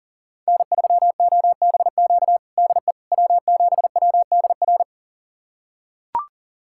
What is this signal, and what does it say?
Morse code, heard as a keyed tone, about 40 wpm, 700 Hz. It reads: N3OBQ DE W7WDR
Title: USA Callsign DE Callsign - 5 Characters - 40wpm